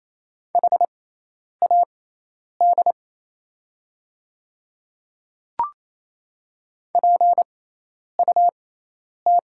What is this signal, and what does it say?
Morse code, heard as a keyed tone, about 27 wpm, spaced as for 10 wpm, 700 Hz. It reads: HAD PUT
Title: 6x Character Spacing - Top 300 Words - 28wpm